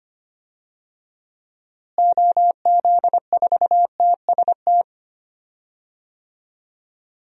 Morse code, heard as a keyed tone, about 25 wpm, 700 Hz. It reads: OZ4TST
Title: DXCC-100 Call Signs - 6 Characters - Repeated 3x - 25wpm